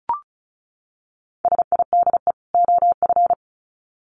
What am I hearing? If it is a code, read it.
Morse code, heard as a keyed tone, about 34 wpm, 700 Hz. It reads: SIDE OF
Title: Sets of 2 Words 35wpm